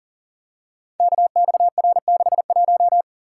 Morse code, heard as a keyed tone, about 39 wpm, 700 Hz. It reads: KXR61
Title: Code Groups - 5 Letter:Numbers - 40wpm